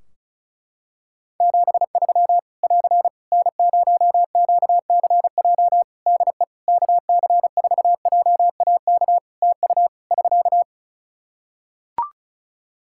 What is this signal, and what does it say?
Morse code, heard as a keyed tone, about 36 wpm, 700 Hz. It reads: 73 <AR> N0QCJ DE KC4JAK TU <SK>
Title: K1USN SST QSO Practice 35wpm